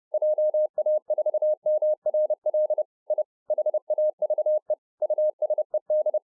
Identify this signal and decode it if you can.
Morse code, heard as a keyed tone, about 30 wpm, 610 Hz. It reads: JA4MRL I HAVE USED